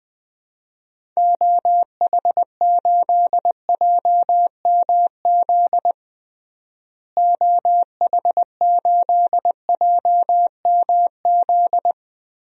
Morse code, heard as a keyed tone, about 20 wpm, 700 Hz. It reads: OH8JMZ OH8JMZ
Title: DXCC-100 Call Signs - 6 Characters - Repeated 2x - 20wpm